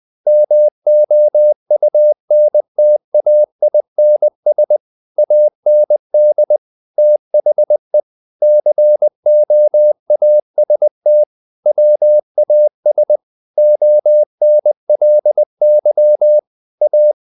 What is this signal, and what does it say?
Morse code, heard as a keyed tone, about 20 wpm, 600 Hz. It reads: MOUNTAINS AND THE COAST WAS ONLY A
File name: the old man and the sea / the old man and the sea part 1 20wpm